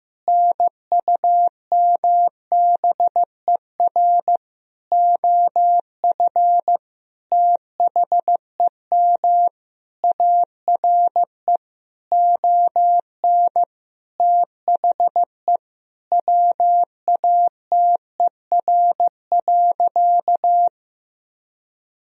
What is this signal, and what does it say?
Morse code, heard as a keyed tone, about 15 wpm, 700 Hz. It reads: NUMBER OF THEM ARE ON THE WATER.
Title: Sentences from Top 1000 Words - Review Part 1 - 15wpm